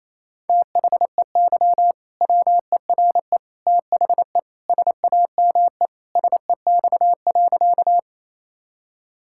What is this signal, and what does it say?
Morse code, heard as a keyed tone, about 27 wpm, 700 Hz. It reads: THEY WERE THE SAME SEX.
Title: Sentences from Top 1600 Words 28wpm